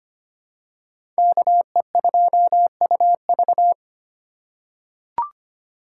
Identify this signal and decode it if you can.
Morse code, heard as a keyed tone, about 25 wpm, 700 Hz. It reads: KE2UV
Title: USA Call Signs - 5 Characters - 2x Character Spacing - 25wpm